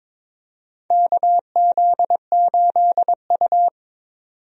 Morse code, heard as a keyed tone, about 22 wpm, 700 Hz. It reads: KZ8U
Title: USA Call Signs - 4 Characters - 22wpm